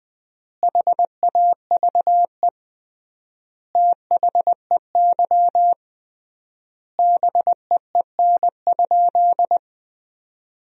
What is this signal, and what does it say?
Morse code, heard as a keyed tone, about 20 wpm, 700 Hz. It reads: HAVE THEY BEEN?
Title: 3x Word-Spacing - Sentences from Top 200 Words - 20wpm